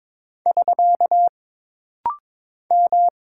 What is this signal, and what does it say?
Morse code, heard as a keyed tone, about 22 wpm, 700 Hz. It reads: <SK> M
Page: This rapid-fire timing is geared to push your ICR (Instant Character Recognition) skills.